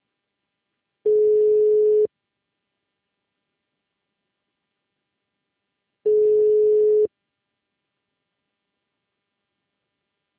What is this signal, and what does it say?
Ringback tone (Europe, 425 Hz).